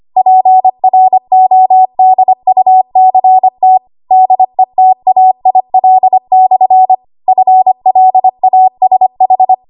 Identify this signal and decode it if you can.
Morse code, heard as a keyed tone, about 25 wpm, 750 Hz. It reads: PRODUCT DETAIL/ FLASH
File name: Archive / 25 WPM / 25WPM